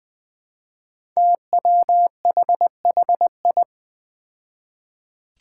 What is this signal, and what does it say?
Morse code, heard as a keyed tone, about 21 wpm, 700 Hz. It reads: TWHHI
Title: Five Challenging Letter-Number Combinations 20wpm